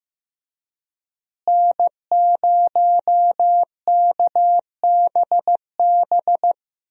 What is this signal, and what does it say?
Morse code, heard as a keyed tone, about 15 wpm, 700 Hz. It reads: N0KBB
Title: USA Call Signs - 5 Characters - Repeated 2x - 15wpm